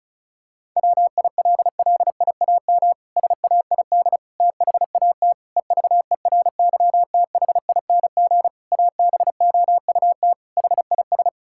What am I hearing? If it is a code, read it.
Morse code, heard as a keyed tone, about 36 wpm, 700 Hz. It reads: WILLIAM SAID THAT EVERYTHING ABOUT HIS
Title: Pangram Challenge v2 - 35wpm